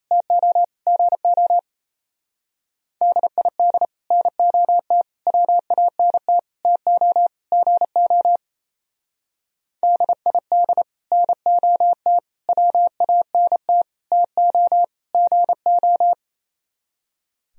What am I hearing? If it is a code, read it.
Morse code, heard as a keyed tone, about 33 wpm, 700 Hz. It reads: TO GO DID NOT WANT TO GO DID NOT WANT TO GO
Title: Speed-Racing - Sets of 5 Words 28wpm